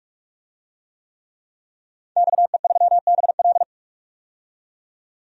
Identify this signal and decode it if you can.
Morse code, heard as a keyed tone, about 46 wpm, 700 Hz. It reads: XE3BL
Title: DXCC-100 Call Signs - 5 Characters - 45wpm